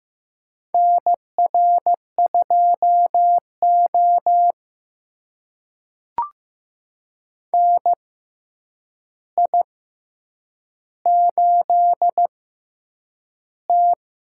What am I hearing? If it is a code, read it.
Morse code, heard as a keyed tone, about 15 wpm, spaced as for 8 wpm, 700 Hz. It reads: NR2O NI8T
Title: USA Call Signs - 6x Character Spacing - 15wpm